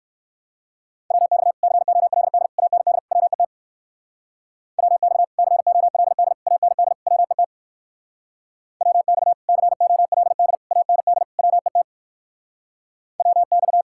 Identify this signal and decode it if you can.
Morse code, heard as a keyed tone, about 62 wpm, 700 Hz. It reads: WX COLD AND WET WX COLD AND WET WX COLD AND WET WX